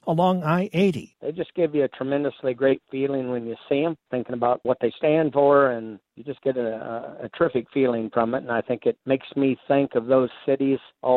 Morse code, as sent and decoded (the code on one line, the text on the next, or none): none